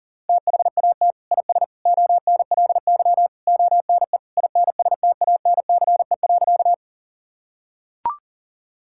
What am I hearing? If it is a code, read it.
Morse code, heard as a keyed tone, about 37 wpm, 700 Hz. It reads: THAT IS ONLY ONE INSTANCE.
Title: Sentences from Top 1000 Words - Review Part 5 - 40wpm